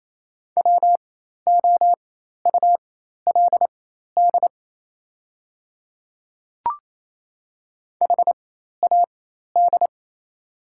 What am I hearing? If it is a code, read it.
Morse code, heard as a keyed tone, about 27 wpm, spaced as for 13 wpm, 690 Hz. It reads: WOULD HAD
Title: 4x Character Spacing - Top 100 Words - 28wpm